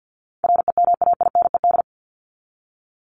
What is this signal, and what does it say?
Morse code, heard as a keyed tone, about 50 wpm, 700 Hz. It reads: REMAINED
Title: ED Suffix - Top 100 Words with Base Words 50wpm